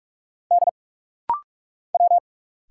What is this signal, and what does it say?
Morse code, heard as a keyed tone, about 46 wpm, 700 Hz. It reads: D W
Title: Single Letter-Number Alt - Rapid-Fire 45wpm